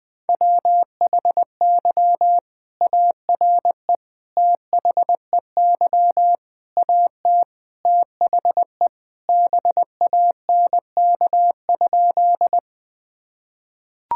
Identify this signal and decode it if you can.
Morse code, heard as a keyed tone, about 21 wpm, 700 Hz. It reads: WHY ARE THEY AT THE BANK?
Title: Sentences from Top 1000 Words - Review Part 6 - 20wpm